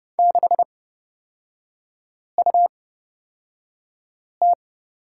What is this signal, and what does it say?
Morse code, heard as a keyed tone, about 31 wpm, spaced as for 5 wpm, 690 Hz. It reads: 6UT
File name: Lesson 37e - Letter x - Three Characters 30wpm